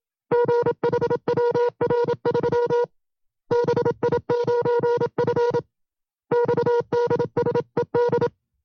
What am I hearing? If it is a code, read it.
Morse code, heard as a keyed tone, about 27 wpm, 480 Hz. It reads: GHWR3 BI9F XDSED
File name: FreeFiles / CW-practice / 25WPM-side2